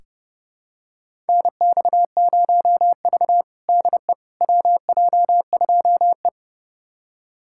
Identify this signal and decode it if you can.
Morse code, heard as a keyed tone, about 30 wpm, 700 Hz. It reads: NX0V DE WJ2E
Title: USA Callsign DE Callsign - 4 Characters - 30wpm